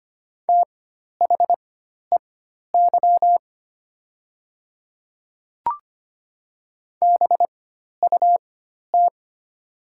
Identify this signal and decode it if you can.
Morse code, heard as a keyed tone, about 25 wpm, spaced as for 12 wpm, 700 Hz. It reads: THEY BUT
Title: 4x Character Spacing - Top 100 Words - 25wpm